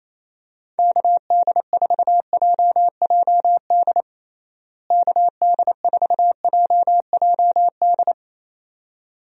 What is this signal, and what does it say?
Morse code, heard as a keyed tone, about 28 wpm, 700 Hz. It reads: KD4JJD KD4JJD
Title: USA Call Signs - 6 Characters - Repeated 2x - 28wpm